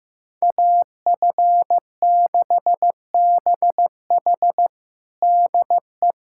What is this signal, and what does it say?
Morse code, heard as a keyed tone, about 15 wpm, 700 Hz. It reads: AF6BH DE
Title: USA Callsign DE Callsign - 5 Characters - 15wpm